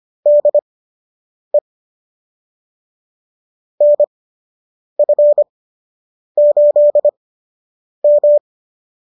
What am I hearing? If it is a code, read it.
Morse code, heard as a keyed tone, about 25 wpm, spaced as for 8 wpm, 590 Hz. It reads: DE NF8M